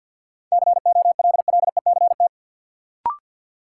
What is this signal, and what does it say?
Morse code, heard as a keyed tone, about 50 wpm, 700 Hz. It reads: COLLECT
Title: Sentences from Top 1600 Words - New Word Drill - 50wpm